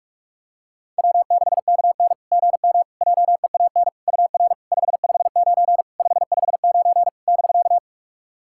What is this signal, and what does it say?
Morse code, heard as a keyed tone, about 42 wpm, 690 Hz. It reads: W6KN GM JEAN UR 559 559 <BK>
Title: SOTA QSO Practice 45wpm